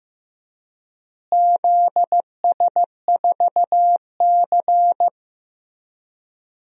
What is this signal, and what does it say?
Morse code, heard as a keyed tone, about 16 wpm, 700 Hz. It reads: ZS4C
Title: DXCC-100 Call Signs - 4 Characters - Repeated 3x - 15wpm